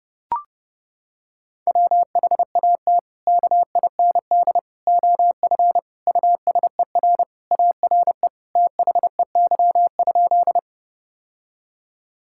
Morse code, heard as a keyed tone, about 32 wpm, 700 Hz. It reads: WHAT KIND OF USER ARE THEY?
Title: Sentences from Top 1000 Words - Review Part 5 - 30wpm